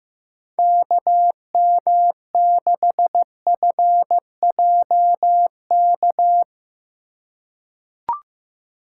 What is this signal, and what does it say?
Morse code, heard as a keyed tone, about 15 wpm, 700 Hz. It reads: KM6FJK
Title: USA Call Signs - 6 Characters - 2x Character Spacing - 15wpm